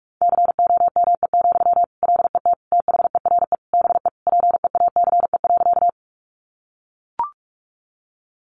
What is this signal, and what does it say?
Morse code, heard as a keyed tone, about 44 wpm, 700 Hz. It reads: COME, LET THERE BE PEACE.
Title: Sentences from Top 1500 Words 45wpm